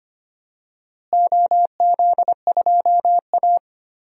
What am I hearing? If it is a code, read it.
Morse code, heard as a keyed tone, about 25 wpm, 700 Hz. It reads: OZ2A
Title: DXCC-100 Call Signs - 4 Characters - Repeated 3x - 25wpm